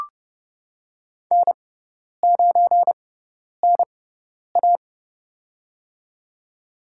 Morse code, heard as a keyed tone, about 29 wpm, spaced as for 10 wpm, 700 Hz. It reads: N9NA
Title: USA Call Signs - 4 Characters - 6x Character Spacing - 30wpm